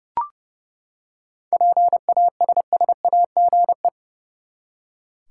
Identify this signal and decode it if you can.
Morse code, heard as a keyed tone, about 30 wpm, 700 Hz. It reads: PASSAGE
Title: Seven Letter Words - 30wpm